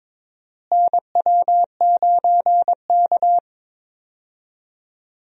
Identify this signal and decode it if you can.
Morse code, heard as a keyed tone, about 21 wpm, 700 Hz. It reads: NW9K
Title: USA Call Signs - 4 Characters - Repeated 2x - 22wpm